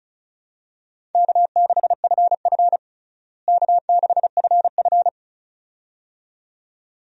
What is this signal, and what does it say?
Morse code, heard as a keyed tone, about 34 wpm, 700 Hz. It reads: K6FF K6FF
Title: USA Call Signs - 4 Characters - Repeated 2x - 35wpm